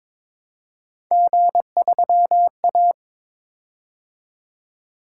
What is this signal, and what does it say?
Morse code, heard as a keyed tone, about 22 wpm, 700 Hz. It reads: G3A
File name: practice / 22wpm / Three Letter-Number Combinations - 22wpm